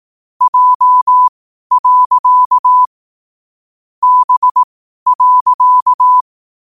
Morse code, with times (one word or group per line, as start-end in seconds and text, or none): 0.4-2.8 J.
4.0-6.2 B.